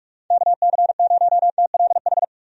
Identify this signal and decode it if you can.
Morse code, heard as a keyed tone, about 46 wpm, 700 Hz. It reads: KC0TLH
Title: USA Call Signs - 6 Characters - 45wpm